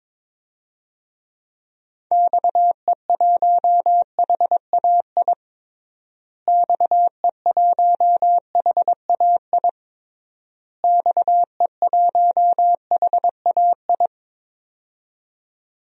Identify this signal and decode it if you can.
Morse code, heard as a keyed tone, about 22 wpm, 700 Hz. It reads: XE1HAI XE1HAI XE1HAI
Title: DXCC-100 Call Signs - 6 Characters - Repeated 3x - 22wpm